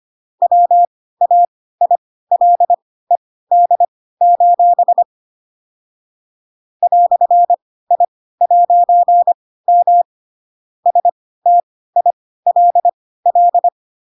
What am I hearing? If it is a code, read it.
Morse code, heard as a keyed tone, about 25 wpm, spaced as for 16 wpm, 700 Hz. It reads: WAILED: "I'M STILL
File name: Pinocchio / Pinocchio ch7 10 25 ik1hge